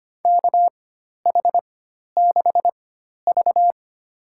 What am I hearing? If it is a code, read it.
Morse code, heard as a keyed tone, about 25 wpm, spaced as for 12 wpm, 700 Hz. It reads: KH6V
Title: USA Call Signs - 4 Characters - 4x Character Spacing - 25wpm